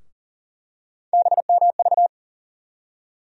Morse code, heard as a keyed tone, about 40 wpm, 700 Hz. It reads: BMV
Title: USA Call Sign Postfixes - 3 Letters 40wpm